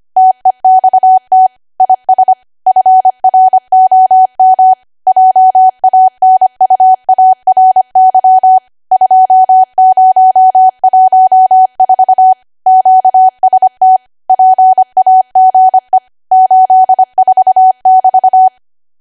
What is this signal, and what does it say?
Morse code, heard as a keyed tone, about 25 wpm, 750 Hz. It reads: TEXT IS FROM JANUARY 2014 QST PAGE 84<BT>
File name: Archive / 25 WPM / 25WPM